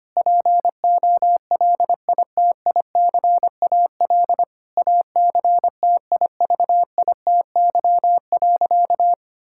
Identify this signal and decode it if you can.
Morse code, heard as a keyed tone, about 25 wpm, 700 Hz. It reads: POLITICAL ACTIVITY.
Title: Sentences from Top 1000 Words - Review Part 5 - 25wpm